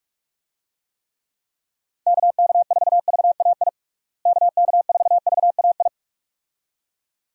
Morse code, heard as a keyed tone, about 46 wpm, 700 Hz. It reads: KK4VAI KK4VAI
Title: USA Call Signs - 6 Characters - Repeated 2x - 45wpm